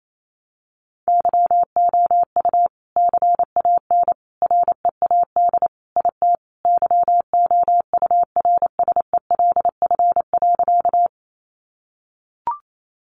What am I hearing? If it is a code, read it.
Morse code, heard as a keyed tone, about 28 wpm, 700 Hz. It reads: YOU CAN READ IT YOURSELF.
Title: Sentences from Top 600 Words 28wpm